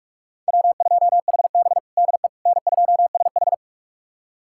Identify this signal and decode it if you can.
Morse code, heard as a keyed tone, about 46 wpm, 700 Hz. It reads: W2HB DE N2SH
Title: USA Callsign DE Callsign - 4 Characters - 45wpm